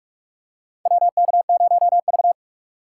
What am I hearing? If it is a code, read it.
Morse code, heard as a keyed tone, about 44 wpm, 700 Hz. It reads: WK0V
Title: USA Call Signs - 4 Characters - 2x Character Spacing - 45wpm